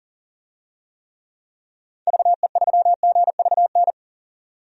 Morse code, heard as a keyed tone, about 38 wpm, 700 Hz. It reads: VE3GVN
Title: DXCC-100 Call Signs - 6 Characters - 40wpm